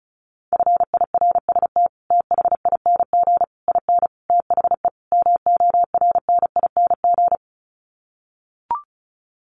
Morse code, heard as a keyed tone, about 34 wpm, 700 Hz. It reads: FIRST THING IN THE MORNING
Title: Sets of 5 Words Encore2 35wpm